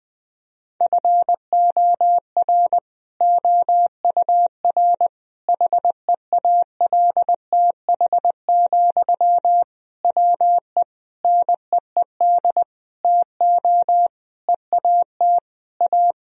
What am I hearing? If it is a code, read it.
Morse code, heard as a keyed tone, about 20 wpm, 700 Hz. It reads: FOR OUR HEALTH, WE NEED TO EAT A
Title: Sentences from Top 1000 Words - Review Part 4 - 20wpm